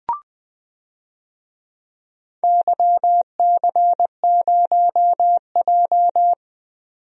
Morse code, heard as a keyed tone, about 20 wpm, 700 Hz. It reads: YC0J
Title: DXCC-100 Call Signs - 4 Characters - 20wpm